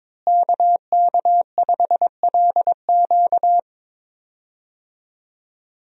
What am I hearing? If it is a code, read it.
Morse code, heard as a keyed tone, about 22 wpm, 700 Hz. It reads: KK5LQ
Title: USA Call Signs - 5 Characters - Repeated 2x - 22wpm